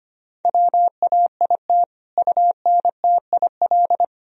Morse code, heard as a keyed tone, about 25 wpm, 700 Hz. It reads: WAIT UNTIL
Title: Sentences from Top 1000 Words - Review Part 1 - 25wpm